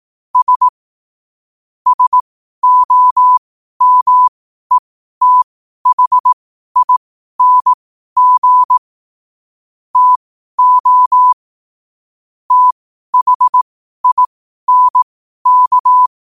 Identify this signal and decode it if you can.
Morse code, heard as a keyed tone, about 18 wpm, spaced as for 12 wpm, 1000 Hz. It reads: S SOMETHING TO THINK